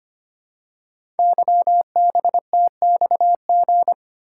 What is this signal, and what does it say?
Morse code, heard as a keyed tone, about 25 wpm, 700 Hz. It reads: YBTXG
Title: Code Groups - 5 Letters - 25wpm